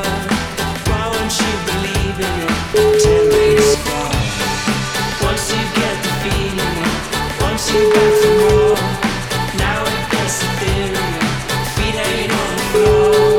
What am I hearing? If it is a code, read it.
Ringback tone (Europe, 425 Hz), with noise.